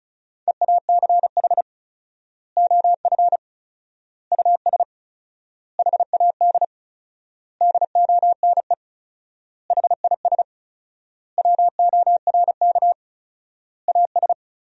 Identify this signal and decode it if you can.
Morse code, heard as a keyed tone, about 34 wpm, 700 Hz. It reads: EACH OF US HAD DONE HIS WORK AS
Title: 4x Word-Spacing - Sentences from Top 500 Words - 35wpm